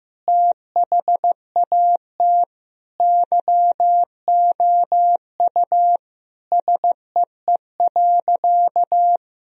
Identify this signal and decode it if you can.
Morse code, heard as a keyed tone, about 15 wpm, 700 Hz. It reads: THAT YOU SEE.
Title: Sentences from Top 1000 Words - Review Part 2 - 15wpm